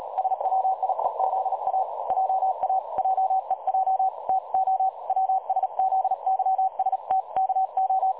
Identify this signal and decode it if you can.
Morse code, heard as a keyed tone, about 38 wpm, 760 Hz. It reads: SWIEZA JA JEJ TO WSZYSTKO
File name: dlazaawansowanych / nagraniazpasma / websdr recording 3565.0kHz